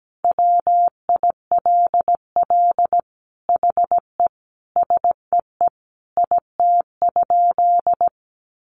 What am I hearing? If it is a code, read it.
Morse code, heard as a keyed tone, about 17 wpm, 700 Hz. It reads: WILL HE SEE IT?
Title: Sentences from Top 100 Words 17wpm